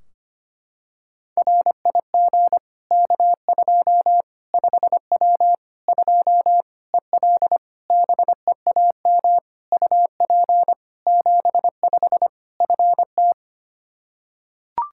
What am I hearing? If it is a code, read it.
Morse code, heard as a keyed tone, about 25 wpm, 700 Hz. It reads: RIG K2 5W 2 EL BEAM UP 75 FT
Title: Practice QSO - Exchange 2 - Info - 25wpm